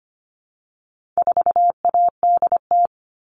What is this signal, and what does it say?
Morse code, heard as a keyed tone, about 25 wpm, 700 Hz. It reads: 4ADT
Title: All Cut Numbers - 4-Digit 25wpm